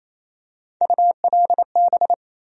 Morse code, heard as a keyed tone, about 28 wpm, 700 Hz. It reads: ULB
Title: USA Call Sign Postfixes - 3 Letters 28wpm